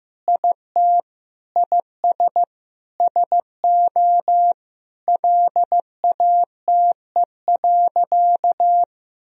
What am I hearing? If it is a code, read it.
Morse code, heard as a keyed tone, about 15 wpm, 700 Hz. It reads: IT IS SO LATE.
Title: Sentences from Top 1000 Words - Review Part 3 - 15wpm